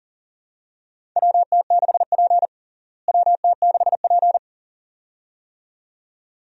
Morse code, heard as a keyed tone, about 40 wpm, 700 Hz. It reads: WT6P WT6P
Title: USA Call Signs - 4 Characters - Repeated 2x - 40wpm